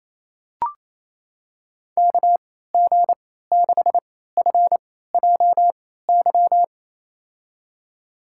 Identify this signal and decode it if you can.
Morse code, heard as a keyed tone, about 28 wpm, spaced as for 16 wpm, 700 Hz. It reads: KG6FJY
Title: USA Call Signs - 6 Characters - 3x Character Spacing - 28wpm